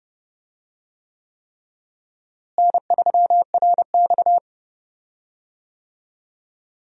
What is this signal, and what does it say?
Morse code, heard as a keyed tone, about 30 wpm, 700 Hz. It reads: N3RX
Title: USA Call Signs - 4 Characters - Repeated 3x - 30wpm